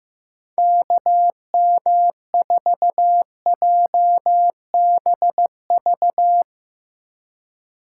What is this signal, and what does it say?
Morse code, heard as a keyed tone, about 16 wpm, 700 Hz. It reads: KM4JBV
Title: Speed Racing - USA 2x3 Call Signs 15wpm